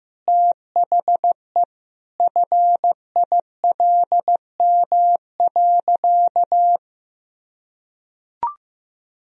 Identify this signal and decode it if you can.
Morse code, heard as a keyed tone, about 15 wpm, 700 Hz. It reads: THE FILM.
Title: Sentences from Top 1000 Words - Review Part 3 - 15wpm